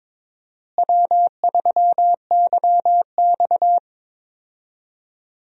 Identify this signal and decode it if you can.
Morse code, heard as a keyed tone, about 22 wpm, 700 Hz. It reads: W3YX
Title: USA Call Signs - 4 Characters - Repeated 3x - 22wpm